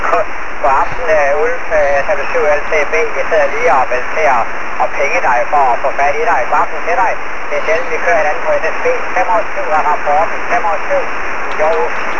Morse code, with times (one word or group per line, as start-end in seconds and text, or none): none